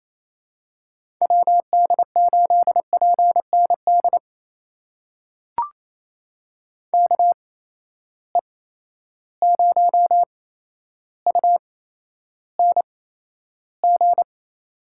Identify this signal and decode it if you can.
Morse code, heard as a keyed tone, about 28 wpm, spaced as for 12 wpm, 700 Hz. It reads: WD8PND KE0UNG
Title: USA Call Signs - 6 Characters - 8x Character Spacing - 28wpm